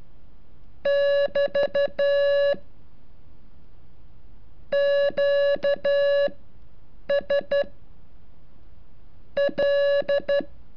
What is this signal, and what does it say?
Morse code, heard as a keyed tone, about 12 wpm, spaced as for 5 wpm, 570 Hz. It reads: <BT> QSL
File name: curso / ejercicio 34